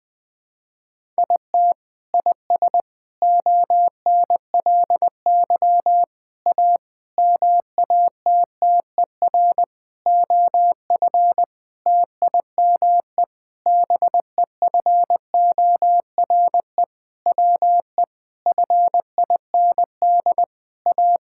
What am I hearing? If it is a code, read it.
Morse code, heard as a keyed tone, about 20 wpm, 700 Hz. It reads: IT IS ONLY A MATTER OF TIME BEFORE WE FIND A
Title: Sentences with 8-Word Phrases - 20wpm